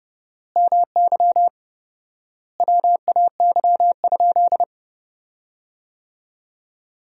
Morse code, heard as a keyed tone, about 32 wpm, 700 Hz. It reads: MY WAY?
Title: 4x Word-Spacing - Sentences from Top 200 Words - 30wpm